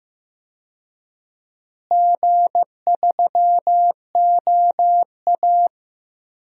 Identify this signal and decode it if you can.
Morse code, heard as a keyed tone, about 16 wpm, 700 Hz. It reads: G3OA
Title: DXCC-100 Call Signs - 4 Characters - Repeated 3x - 15wpm